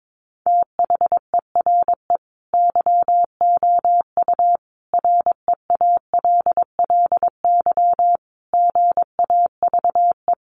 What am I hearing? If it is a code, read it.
Morse code, heard as a keyed tone, about 22 wpm, 700 Hz. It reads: THERE YOU REALLY GAVE
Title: Sentences from Top 900 Words 22wpm